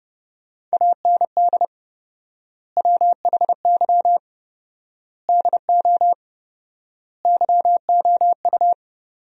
Morse code, heard as a keyed tone, about 30 wpm, 700 Hz. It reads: AND WHY DO YOU
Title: 4x Word-Spacing - Sentences from Top 400 Words - 30wpm